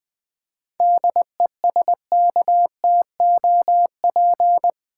Morse code, heard as a keyed tone, about 20 wpm, 690 Hz. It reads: DESKTOP
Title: 7 Letter Words - First 3 Letters Given - 20wpm